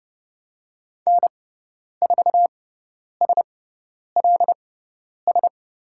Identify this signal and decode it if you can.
Morse code, heard as a keyed tone, about 30 wpm, spaced as for 10 wpm, 700 Hz. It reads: N4SLS
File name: Lesson 16g - Letter c - Call Signs 30wpm